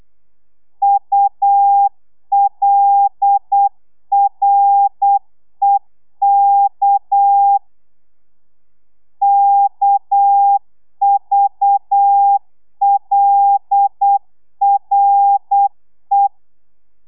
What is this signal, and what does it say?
Morse code, heard as a keyed tone, about 8 wpm, 800 Hz. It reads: ULREK KVLRE